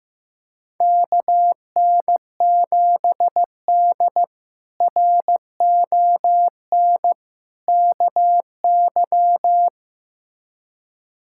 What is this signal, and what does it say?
Morse code, heard as a keyed tone, about 15 wpm, 700 Hz. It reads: KN7D RON KY
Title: K1USN SST QSO Practice - 15wpm